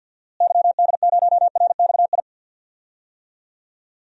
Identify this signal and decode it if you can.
Morse code, heard as a keyed tone, about 50 wpm, 700 Hz. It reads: YD0RXI
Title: DXCC-100 Call Signs - 6 Characters - Repeated 2x - 50wpm